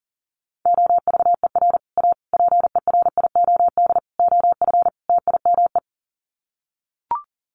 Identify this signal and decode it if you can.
Morse code, heard as a keyed tone, about 40 wpm, 700 Hz. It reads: OVER A PERIOD OF TIME
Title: Sets of 5 Words Encore2 40wpm